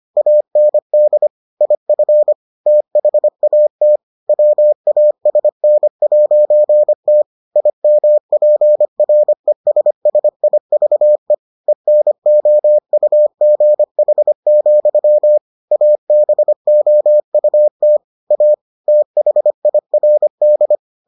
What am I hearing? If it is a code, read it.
Morse code, heard as a keyed tone, about 25 wpm, 600 Hz. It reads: AND IF THAT WASN'T IMPRESSIVE ENOUGH, ABOUT A THIRD